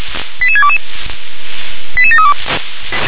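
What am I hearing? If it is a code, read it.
Five-tone selective call (ZVEI), heard as quick runs of five tones, tones of about 70 ms. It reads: 88622 88732
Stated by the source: ZVEI five-tone selective call received by SDR.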